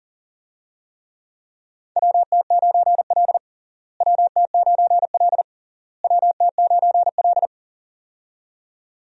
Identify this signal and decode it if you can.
Morse code, heard as a keyed tone, about 40 wpm, 700 Hz. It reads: WT9L WT9L WT9L
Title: USA Call Signs - 4 Characters - Repeated 3x - 40wpm